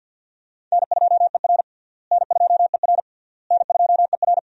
Morse code, heard as a keyed tone, about 52 wpm, 700 Hz. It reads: N2ER N2ER N2ER
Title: USA Call Signs - 4 Characters - Repeated 3x - 50wpm